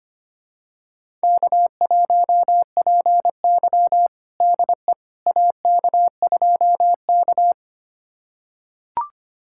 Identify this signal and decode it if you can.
Morse code, heard as a keyed tone, about 25 wpm, 700 Hz. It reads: K1PY DE AK2K
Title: USA Callsign DE Callsign - 4 Characters - 25wpm